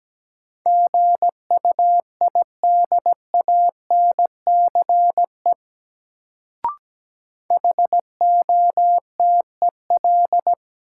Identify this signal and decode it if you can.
Morse code, heard as a keyed tone, about 17 wpm, 700 Hz. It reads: GUIDANCE HOTEL
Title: Sentences from Top 1600 Words - New Word Drill - 17wpm